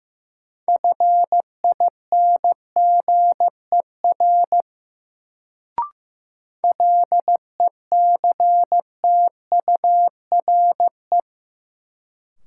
Morse code, heard as a keyed tone, about 15 wpm, 700 Hz. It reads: FINGER LECTURE
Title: Sentences from Top 1600 Words - New Word Drill Alt - 15wpm